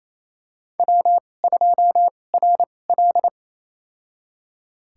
Morse code, heard as a keyed tone, about 28 wpm, spaced as for 20 wpm, 700 Hz. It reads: W2RL
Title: USA Call Signs - 4 Characters - 2x Character Spacing - 28wpm